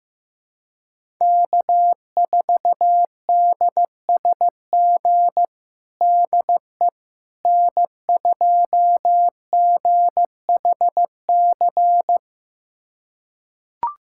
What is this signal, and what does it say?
Morse code, heard as a keyed tone, about 15 wpm, 700 Hz. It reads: K4DSG DE N2GHC
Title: USA Callsign DE Callsign - 5 Characters - 15wpm